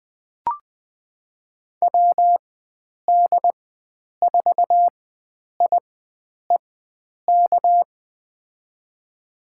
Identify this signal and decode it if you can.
Morse code, heard as a keyed tone, about 20 wpm, spaced as for 9 wpm, 700 Hz. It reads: WD4IEK
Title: USA Call Signs - 6 Characters - 4x Character Spacing - 20wpm